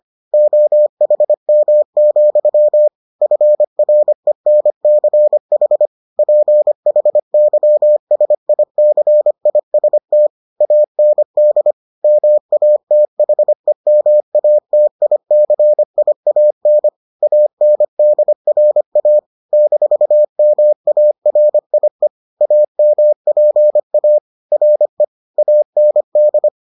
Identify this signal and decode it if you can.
Morse code, heard as a keyed tone, about 25 wpm, 600 Hz. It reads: OHM, FRENCH PHYSICIST AND MATHEMATICIAN ANDRA -MARIE AMPA RE AND